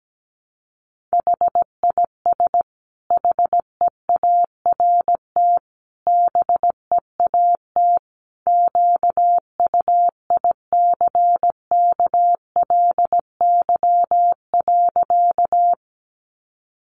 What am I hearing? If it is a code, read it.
Morse code, heard as a keyed tone, about 17 wpm, 700 Hz. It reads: HIS HEART BEAT QUICKLY.
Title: Sentences from Top 1300 Words 17wpm